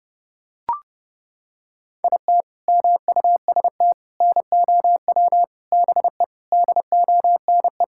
Morse code, heard as a keyed tone, about 32 wpm, 700 Hz. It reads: IT MUST NOW BE DONE
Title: Sentences from Top 1000 Words - Review Part 5 - 30wpm